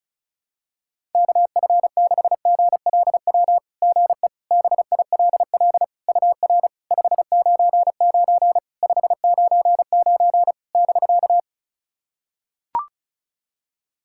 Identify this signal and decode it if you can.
Morse code, heard as a keyed tone, about 35 wpm, 700 Hz. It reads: KF6GLW GE BILL UR 599 599 <BK>
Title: SOTA QSO Practice 35wpm